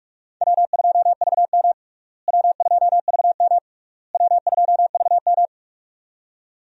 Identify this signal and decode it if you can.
Morse code, heard as a keyed tone, about 46 wpm, 700 Hz. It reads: W2VM W2VM W2VM
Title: USA Call Signs - 4 Characters - Repeated 3x - 45wpm